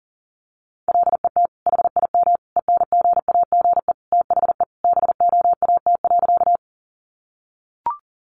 Morse code, heard as a keyed tone, about 40 wpm, 700 Hz. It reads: LET HIM ENGAGE THE BOAT.
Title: Sentences from Top 1500 Words 40wpm